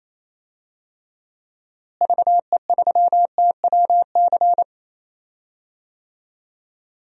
Morse code, heard as a keyed tone, about 28 wpm, 700 Hz. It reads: VE3TWC
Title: DXCC-100 Call Signs - 6 Characters - Repeated 3x - 28wpm